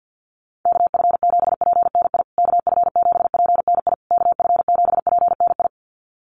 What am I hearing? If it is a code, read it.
Morse code, heard as a keyed tone, about 48 wpm, 700 Hz. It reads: KF7PNI KF7PNI KF7PNI
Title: USA Call Signs Repeated 3x 50wpm